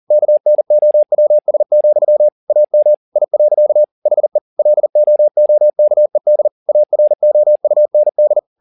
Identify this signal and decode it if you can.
Morse code, heard as a keyed tone, about 40 wpm, 600 Hz. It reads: KNOWS, AM I. HE LOOKED AROUND